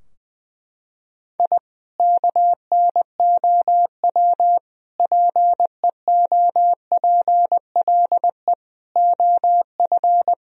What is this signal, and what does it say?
Morse code, heard as a keyed tone, about 20 wpm, 700 Hz. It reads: I KNOW PEOPLE OF
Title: Sentences from Top 1000 Words - Review Part 4 - 20wpm